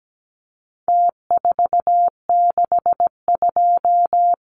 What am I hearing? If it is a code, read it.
Morse code, heard as a keyed tone, about 17 wpm, 700 Hz. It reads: T462
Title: Numbers - Contest Style - 4-Digit 17wpm